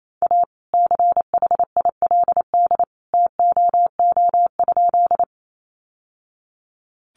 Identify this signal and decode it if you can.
Morse code, heard as a keyed tone, about 28 wpm, 700 Hz. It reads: A CHILD TOO?
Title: Sentences from Top 800 Words 28wpm